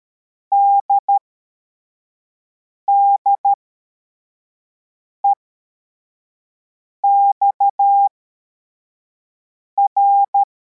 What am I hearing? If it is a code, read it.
Morse code, heard as a keyed tone, about 13 wpm, spaced as for 4 wpm, 800 Hz. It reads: DDEXR